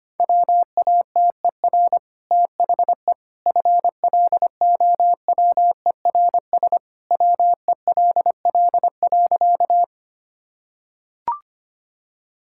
Morse code, heard as a keyed tone, about 25 wpm, 700 Hz. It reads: WATER THE FLOWERS WELL.
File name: Easy Sentences with Hints - 25wpm